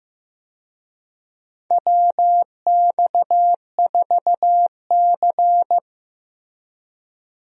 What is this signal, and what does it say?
Morse code, heard as a keyed tone, about 15 wpm, 700 Hz. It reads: WX4C
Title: USA Call Signs - 4 Characters - Repeated 2x - 15wpm